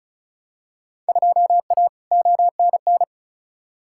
Morse code, heard as a keyed tone, about 36 wpm, 700 Hz. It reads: 2A ONN
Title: Field Day Exchanges Encore - 35wpm